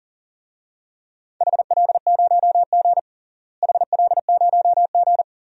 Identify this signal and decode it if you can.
Morse code, heard as a keyed tone, about 37 wpm, 700 Hz. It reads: HL0G HL0G
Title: DXCC-100 Call Signs - 4 Characters - Repeated 3x - 40wpm